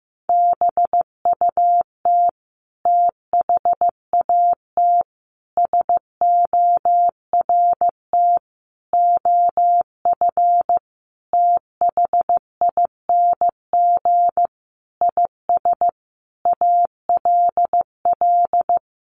Morse code, heard as a keyed tone, about 15 wpm, 700 Hz. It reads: BUT THAT SORT OF THING IS ALL
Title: Sentences from Top 1300 Words 15wpm